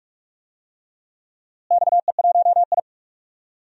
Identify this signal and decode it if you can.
Morse code, heard as a keyed tone, about 42 wpm, 700 Hz. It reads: XE1I
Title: DXCC-100 Call Signs - 4 Characters - Repeated 3x - 45wpm